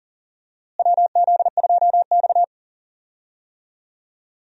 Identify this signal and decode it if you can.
Morse code, heard as a keyed tone, about 40 wpm, 700 Hz. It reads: WZ2X
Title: USA Call Signs - 4 Characters - Repeated 2x - 40wpm